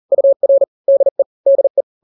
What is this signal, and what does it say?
Morse code, heard as a keyed tone, about 40 wpm, 550 Hz. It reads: UR DE DE
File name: n-grams / 2-grams 40 35 WPM